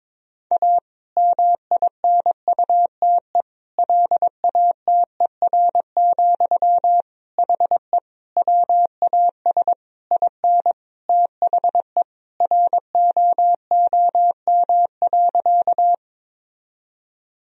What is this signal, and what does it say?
Morse code, heard as a keyed tone, about 22 wpm, 700 Hz. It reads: A MINUTE LATER, HE WAS IN THE ROOM.
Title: Sentences from Top 1000 Words - Review Part 4 - 22wpm